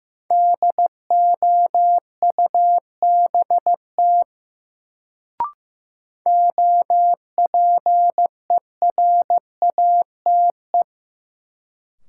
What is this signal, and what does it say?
Morse code, heard as a keyed tone, about 15 wpm, 700 Hz. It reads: DOUBT OPERATE
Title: Sentences from Top 1500 Words - New Word Drill - 15wpm